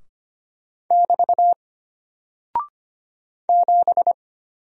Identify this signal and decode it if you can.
Morse code, heard as a keyed tone, about 25 wpm, 700 Hz. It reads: <BT> 7
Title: Single Character (Letter:Number:Punctuation:Prosign) Alt 25wpm